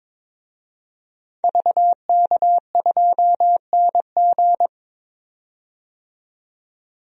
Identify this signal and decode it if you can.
Morse code, heard as a keyed tone, about 22 wpm, 700 Hz. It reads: VK2NG
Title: DXCC-100 Call Signs - 5 Characters - 22wpm